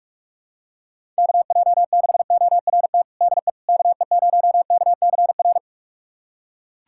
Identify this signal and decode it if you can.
Morse code, heard as a keyed tone, about 42 wpm, 700 Hz. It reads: KJ6ORT DE KE0KCR
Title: USA Callsign DE Callsign - 6 Characters - 45wpm